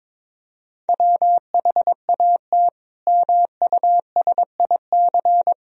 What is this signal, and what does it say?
Morse code, heard as a keyed tone, about 22 wpm, 700 Hz. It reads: WHAT MUSIC
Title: Sentences from Top 1000 Words - Review Part 1 - 22wpm